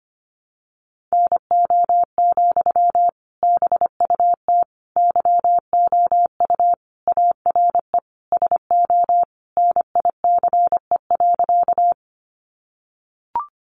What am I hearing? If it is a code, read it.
Morse code, heard as a keyed tone, about 25 wpm, 700 Hz. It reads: NO, BUT YOU ARE SO NICE.
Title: Sentences from Top 1100 Words 25wpm